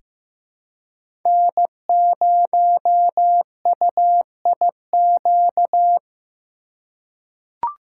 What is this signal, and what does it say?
Morse code, heard as a keyed tone, about 15 wpm, 700 Hz. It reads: N0UIQ
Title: USA Call Signs - 5 Characters - 2x Character Spacing - 15wpm